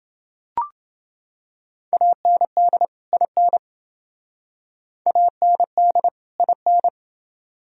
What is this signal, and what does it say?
Morse code, heard as a keyed tone, about 27 wpm, 700 Hz. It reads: AND IN AND IN